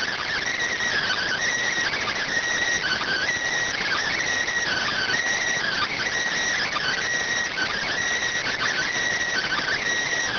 Emergency Alert System (SAME) tones (header bursts), heard as harsh buzzing bursts. It is heard from 0.9 to 9.8 s.